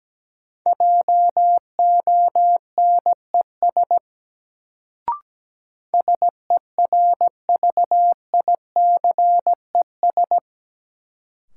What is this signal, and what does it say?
Morse code, heard as a keyed tone, about 17 wpm, 700 Hz. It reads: JONES SERVICES
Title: Sets of 5 Words - New Word Drill - 17wpm